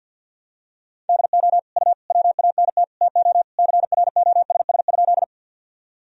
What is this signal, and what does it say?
Morse code, heard as a keyed tone, about 50 wpm, 700 Hz. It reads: DO U WANT TO CROSS?
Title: Sentences from Top 300 Words and Abbreviations 50wpm